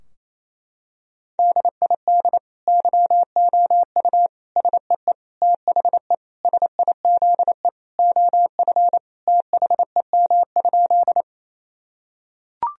Morse code, heard as a keyed tone, about 28 wpm, 700 Hz. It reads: DID YOU SEE THE SIZE OF THEM?
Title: Sentences from Top 1000 Words - Review Part 2 - 28wpm